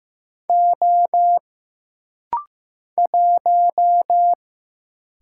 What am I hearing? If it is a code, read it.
Morse code, heard as a keyed tone, about 15 wpm, 700 Hz. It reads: O 1
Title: Single Letter-Number Alt - Rapid-Fire 15wpm